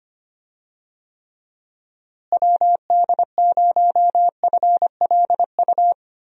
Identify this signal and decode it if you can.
Morse code, heard as a keyed tone, about 25 wpm, 700 Hz. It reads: WD0FLU
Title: USA Call Signs - 6 Characters - Repeated 3x - 25wpm